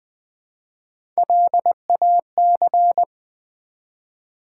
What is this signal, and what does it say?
Morse code, heard as a keyed tone, about 20 wpm, 700 Hz. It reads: LAC
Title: Build A Word - 1 Letter At A Time - 4-Letter Words - 20wpm